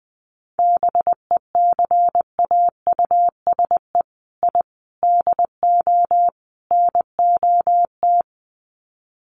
Morse code, heard as a keyed tone, about 20 wpm, 700 Hz. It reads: BECAUSE I DO NOT
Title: Sets of 4 Words 20wpm